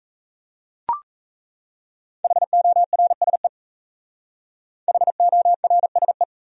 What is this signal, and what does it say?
Morse code, heard as a keyed tone, about 35 wpm, 700 Hz. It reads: HORSE HORSE
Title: Speed-Racing Top 300 Words 28wpm